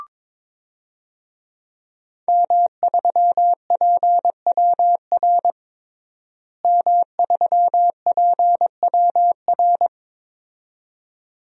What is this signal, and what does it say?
Morse code, heard as a keyed tone, about 22 wpm, 700 Hz. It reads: M3PWR M3PWR
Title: DXCC-100 Call Signs - 5 Characters - Repeated 2x - 22wpm